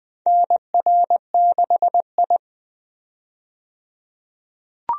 Morse code, heard as a keyed tone, about 20 wpm, 700 Hz. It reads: NR6I
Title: USA Call Signs - 4 Characters - 20wpm